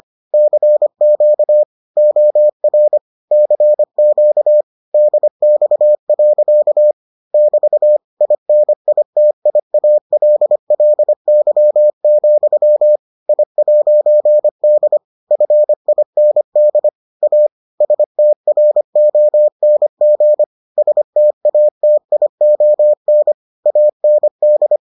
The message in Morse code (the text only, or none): CQ OR CQ DX. <BT> INITIALLY, I'D FIND A STRONG STATION AND